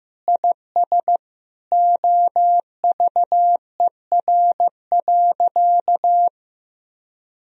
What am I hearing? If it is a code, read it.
Morse code, heard as a keyed tone, about 15 wpm, 700 Hz. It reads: IS OVER.
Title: Sentences from Top 1000 Words - Review Part 1 - 15wpm